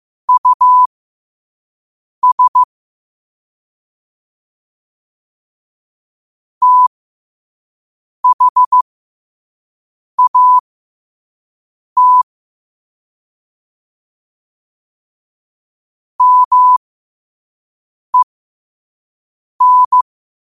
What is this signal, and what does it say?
Morse code, heard as a keyed tone, about 15 wpm, spaced as for 5 wpm, 1000 Hz. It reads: US THAT MEN
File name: 5wpm